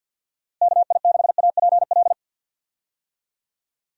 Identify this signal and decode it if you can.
Morse code, heard as a keyed tone, about 50 wpm, 700 Hz. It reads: KI6APL
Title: USA Call Signs - 6 Characters - Repeated 3x - 50wpm